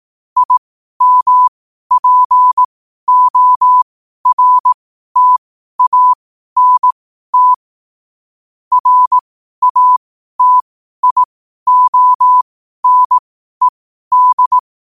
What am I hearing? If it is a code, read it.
Morse code, heard as a keyed tone, about 18 wpm, spaced as for 12 wpm, 1000 Hz. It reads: IMPORTANT RATIONED
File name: qotdfiles / 12wpm